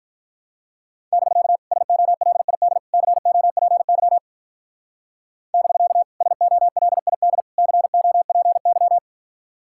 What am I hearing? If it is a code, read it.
Morse code, heard as a keyed tone, about 43 wpm, 700 Hz. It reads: <BK> SOLID COPY <BK> SOLID COPY
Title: Speed-Racing Common QSO Phrases - 35wpm